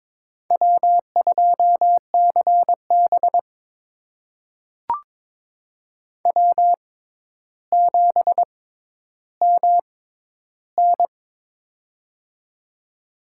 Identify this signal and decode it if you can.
Morse code, heard as a keyed tone, about 22 wpm, spaced as for 11 wpm, 700 Hz. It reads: W2CB W7MN
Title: USA Call Signs - 6x Character Spacing - 22wpm